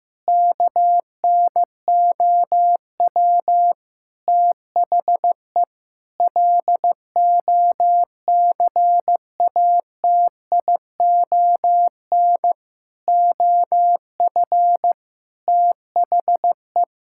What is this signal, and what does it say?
Morse code, heard as a keyed tone, about 16 wpm, 700 Hz. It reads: KNOW THE LOCATION OF THE
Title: Sentences from Top 1000 Words - Review Part 6 - 15wpm